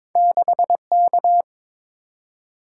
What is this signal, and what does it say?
Morse code, heard as a keyed tone, about 22 wpm, 700 Hz. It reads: 6K
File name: practice / 22wpm / Two Letter-Number Combinations - 22wpm